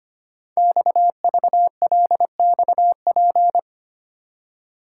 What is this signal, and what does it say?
Morse code, heard as a keyed tone, about 25 wpm, 700 Hz. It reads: XVLXP
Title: Code Groups - 5 Letters - 25wpm